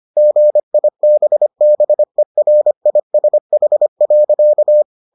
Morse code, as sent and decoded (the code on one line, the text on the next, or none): --. .. -... -... .-. .. ... .... .-.-.-
GIBBERISH.